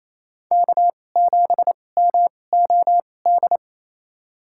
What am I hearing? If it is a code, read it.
Morse code, heard as a keyed tone, about 28 wpm, spaced as for 20 wpm, 700 Hz. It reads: K7MOD
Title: USA Call Signs - 5 Characters - 2x Character Spacing - 28wpm